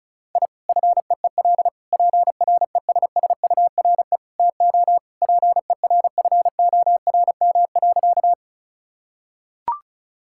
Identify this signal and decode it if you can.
Morse code, heard as a keyed tone, about 36 wpm, 700 Hz. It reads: I FEEL PRESSURE TO PERFORM.
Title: Sentences from Top 1000 Words - Review Part 6 - 35wpm